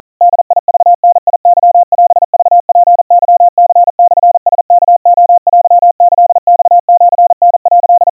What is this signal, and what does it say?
Morse code (Japanese Wabun code), heard as a keyed tone, about 40 wpm, 700 Hz. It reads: ボクダケオクツケニモラワレテキマシタ」
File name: kawa / Morse sample / kamyu 150 cpm 700 Hz Crispy